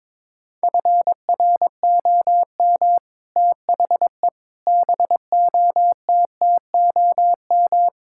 Morse code, heard as a keyed tone, about 21 wpm, 700 Hz. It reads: FROM THE BOTTOM
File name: practice / 22wpm / Sets of 6 Words - 22wpm